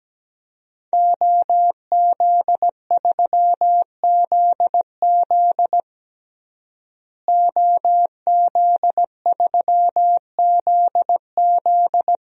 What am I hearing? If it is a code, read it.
Morse code, heard as a keyed tone, about 17 wpm, 700 Hz. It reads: OZ3ZZ OZ3ZZ
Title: DXCC-100 Call Signs - 5 Characters - Repeated 3x - 17wpm